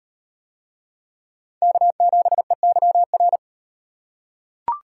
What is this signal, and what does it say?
Morse code, heard as a keyed tone, about 38 wpm, 700 Hz. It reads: K7EYR